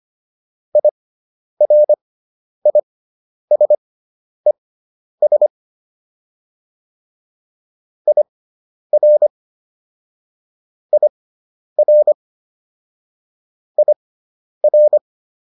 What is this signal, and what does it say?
Morse code, heard as a keyed tone, about 25 wpm, spaced as for 10 wpm, 600 Hz. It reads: IRISES IR IR IR
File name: IR6 10